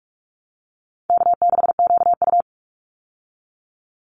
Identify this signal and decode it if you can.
Morse code, heard as a keyed tone, about 44 wpm, 700 Hz. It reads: K6QU
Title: Call Signs Repeated 2x - 45wpm